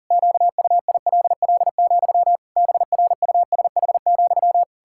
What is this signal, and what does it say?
Morse code, heard as a keyed tone, about 40 wpm, 700 Hz. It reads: QUILL, BRUSH,
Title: Pangram Challenge v2 - 40wpm